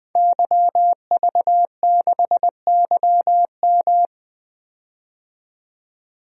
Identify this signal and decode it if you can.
Morse code, heard as a keyed tone, about 21 wpm, 700 Hz. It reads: YV6YM
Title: DXCC-100 Call Signs - 5 Characters - 20wpm